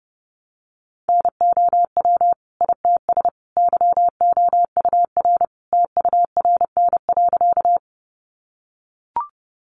Morse code, heard as a keyed tone, about 30 wpm, 700 Hz. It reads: NOW ITS YOUR TURN.
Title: Sentences from Top 600 Words 30wpm